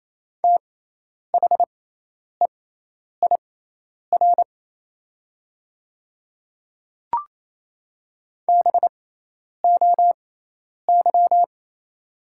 Morse code, heard as a keyed tone, about 28 wpm, spaced as for 10 wpm, 700 Hz. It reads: THEIR BOY
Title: 6x Character Spacing - Top 300 Words - 28wpm